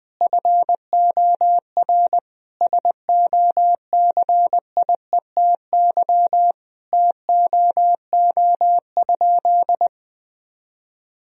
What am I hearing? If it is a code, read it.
Morse code, heard as a keyed tone, about 21 wpm, 700 Hz. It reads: FOR SOCIETY TOO?
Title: Sentences from Top 1000 Words - Review Part 4 - 20wpm